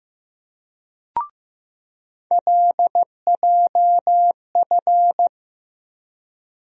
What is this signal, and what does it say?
Morse code, heard as a keyed tone, about 15 wpm, 700 Hz. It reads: LJF
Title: USA Call Sign Postfixes - 3 Letters 15wpm